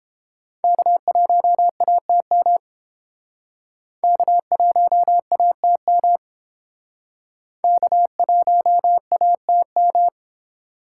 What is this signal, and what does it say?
Morse code, heard as a keyed tone, about 28 wpm, 700 Hz. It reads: K1ATM K1ATM K1ATM